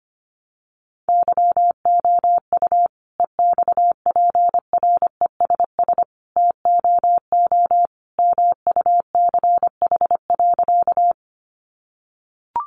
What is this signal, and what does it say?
Morse code, heard as a keyed tone, about 25 wpm, 690 Hz. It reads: YOU EXPRESS TOO MUCH.
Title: Sentences from Top 1100 Words 25wpm